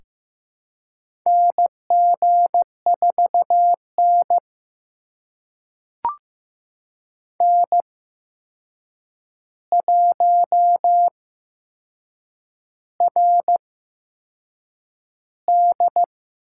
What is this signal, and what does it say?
Morse code, heard as a keyed tone, about 15 wpm, spaced as for 6 wpm, 700 Hz. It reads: NG4N N1RD
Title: USA Call Signs - 4 Characters - 8x Character Spacing - 15wpm